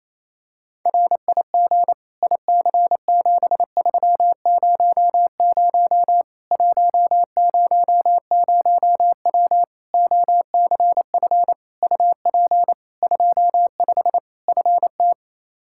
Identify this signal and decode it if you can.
Morse code, heard as a keyed tone, about 28 wpm, 700 Hz. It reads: RIG IC7300 100W OCF UP 25 FT
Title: Practice QSO - Exchange 2 - Info - 28wpm